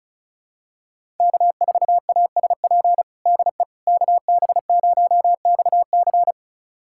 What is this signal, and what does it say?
Morse code, heard as a keyed tone, about 36 wpm, 700 Hz. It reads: K4ASP DE KB0XC
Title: USA Callsign DE Callsign - 5 Characters - 35wpm